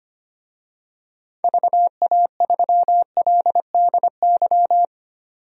Morse code, heard as a keyed tone, about 25 wpm, 700 Hz. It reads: VA3LDY
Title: DXCC-100 Call Signs - 6 Characters - 25wpm